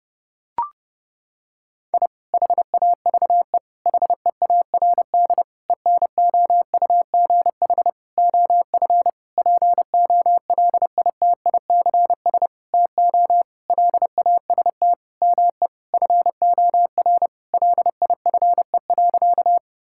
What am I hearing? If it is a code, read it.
Morse code, heard as a keyed tone, about 30 wpm, 700 Hz. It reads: I HAVE HEARD ENOUGH OF POLITICS TO LAST ME FOR LIFE.
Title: Sentences from Top 1000 Words - Review Part 6 - 30wpm